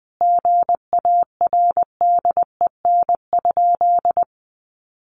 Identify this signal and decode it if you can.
Morse code, heard as a keyed tone, about 19 wpm, 700 Hz. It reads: GARDEN?
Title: Sentences from Top 700 Words 20wpm